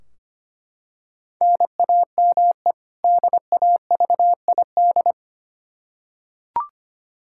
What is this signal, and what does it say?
Morse code, heard as a keyed tone, about 25 wpm, 700 Hz. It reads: NAME DAVID
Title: K1USN SST QSO Practice 25wpm